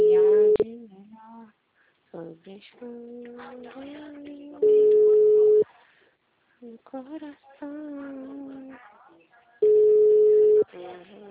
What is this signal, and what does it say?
Ringback tone (Europe, 425 Hz), with speech.